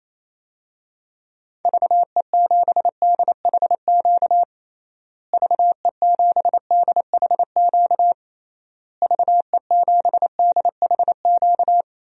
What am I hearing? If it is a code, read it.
Morse code, heard as a keyed tone, about 27 wpm, 700 Hz. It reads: VE7DHQ VE7DHQ VE7DHQ
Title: DXCC-100 Call Signs - 6 Characters - Repeated 3x - 28wpm